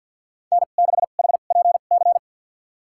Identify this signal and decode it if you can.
Morse code, heard as a keyed tone, about 50 wpm, spaced as for 36 wpm, 700 Hz. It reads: N6HPC
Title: USA Call Signs - 5 Characters - 2x Character Spacing - 50wpm